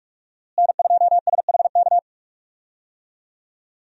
Morse code, heard as a keyed tone, about 41 wpm, 700 Hz. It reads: N2SHK
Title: USA Call Signs - 5 Characters - Repeated 3x - 45wpm